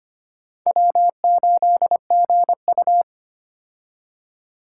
Morse code, heard as a keyed tone, about 25 wpm, 700 Hz. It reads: W8GU
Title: K1USN SST QSO Practice - 25wpm